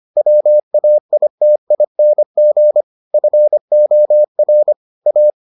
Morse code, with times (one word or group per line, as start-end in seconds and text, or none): 0.2-2.8 WAITING
3.1-4.7 FOR
5.1-5.3 A